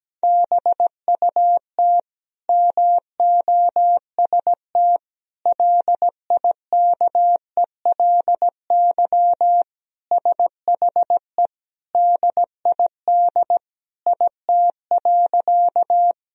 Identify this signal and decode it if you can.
Morse code, heard as a keyed tone, about 17 wpm, 700 Hz. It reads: BUT MOST LIKELY SHE DID IT.
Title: Sentences from Top 1000 Words - Review Part 4 - 17wpm